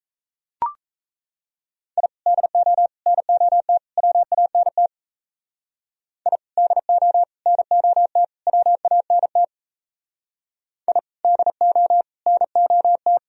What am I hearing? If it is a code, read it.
Morse code, heard as a keyed tone, about 37 wpm, 700 Hz. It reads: I DO NOT WANT I DO NOT WANT I DO NOT